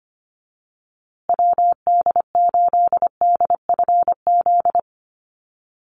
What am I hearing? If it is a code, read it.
Morse code, heard as a keyed tone, about 25 wpm, 700 Hz. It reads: WD8DFZ
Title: USA Call Signs Repeated 2x 25wpm